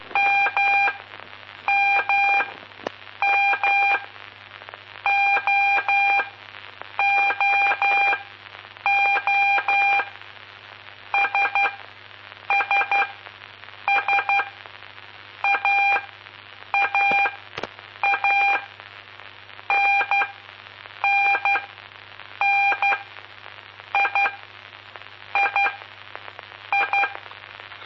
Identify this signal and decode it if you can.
Morse code, heard as a keyed tone, about 12 wpm, spaced as for 7 wpm, 2400 Hz, with noise. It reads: MMM OOO SSS AAA NNN III
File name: aulas de telegrafia